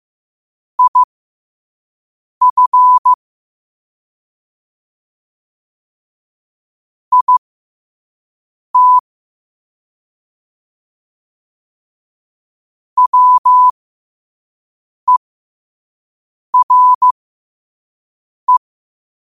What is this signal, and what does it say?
Morse code, heard as a keyed tone, about 15 wpm, spaced as for 5 wpm, 1000 Hz. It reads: IF IT WERE
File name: qotdfiles / 5wpm